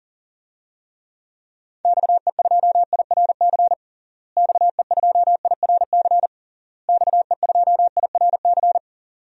Morse code, heard as a keyed tone, about 40 wpm, 700 Hz. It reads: XE2IRC XE2IRC XE2IRC
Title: DXCC-100 Call Signs - 6 Characters - Repeated 3x - 40wpm